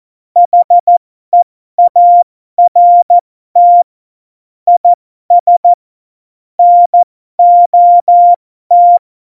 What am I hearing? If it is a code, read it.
Morse code, heard as a keyed tone, about 14 wpm, spaced as for 12 wpm, 700 Hz. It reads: HEART IS NOT
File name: Morsekurs / Witze